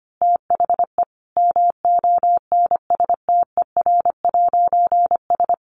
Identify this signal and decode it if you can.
Morse code, heard as a keyed tone, about 25 wpm, 700 Hz. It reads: THE MONSTER'S
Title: A Sound Of Thunder 25wpm